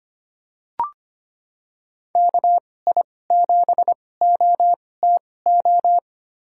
Morse code, heard as a keyed tone, about 25 wpm, spaced as for 18 wpm, 700 Hz. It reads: KI7OTO
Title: USA Call Signs - 6 Characters - 2x Character Spacing - 25wpm